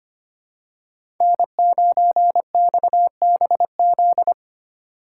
Morse code, heard as a keyed tone, about 25 wpm, 700 Hz. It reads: N9XBZ
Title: USA Call Signs - 5 Characters - 4x Character Spacing - 25wpm